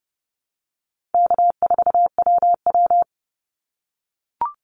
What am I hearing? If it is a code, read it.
Morse code, heard as a keyed tone, about 30 wpm, 700 Hz. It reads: K4WW
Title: Easy Call Signs - 30wpm